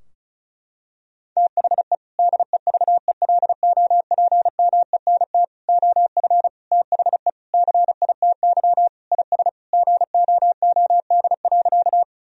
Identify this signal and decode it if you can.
Morse code, heard as a keyed tone, about 36 wpm, 700 Hz. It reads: THE DEVELOPMENT OF THE CITY IS GOOD.
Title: Sentences from Top 1000 Words - Review Part 4 - 35wpm